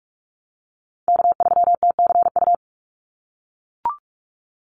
Morse code, heard as a keyed tone, about 44 wpm, 700 Hz. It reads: K3TCU
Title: NA SOTA Chaser Callsigns 45wpm